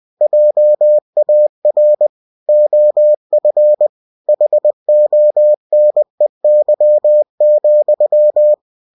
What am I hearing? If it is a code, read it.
Morse code, heard as a keyed tone, about 20 wpm, 600 Hz. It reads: JAR OF HONEY,
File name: cw-books / winnie-the-pooh / winnie-20wpm-page-0036